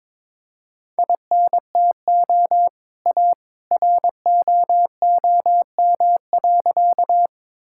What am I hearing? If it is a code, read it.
Morse code, heard as a keyed tone, about 22 wpm, 700 Hz. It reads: INTO A ROOM.
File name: practice / 22wpm / Sentences with 2-Word Phrases - 22wpm